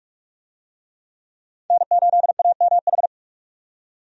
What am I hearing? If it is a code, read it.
Morse code, heard as a keyed tone, about 44 wpm, 700 Hz. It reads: N8AMH
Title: USA Call Signs - 5 Characters - 45wpm